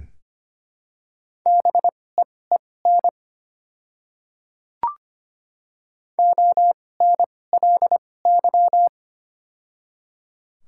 Morse code, heard as a keyed tone, about 25 wpm, spaced as for 18 wpm, 700 Hz. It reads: BEEN ONLY
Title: 2x Character Spacing - Top 200 Words - 25wpm